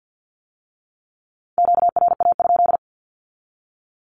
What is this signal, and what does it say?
Morse code, heard as a keyed tone, about 50 wpm, 700 Hz. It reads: QRA?
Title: Q-Codes with Spoken Meaning - 50wpm